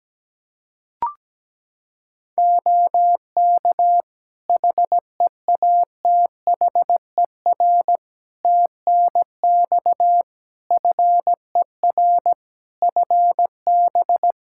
Morse code, heard as a keyed tone, about 17 wpm, 700 Hz. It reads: OK HEATHER TNX FER FB
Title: K1USN SST QSO Practice 17wpm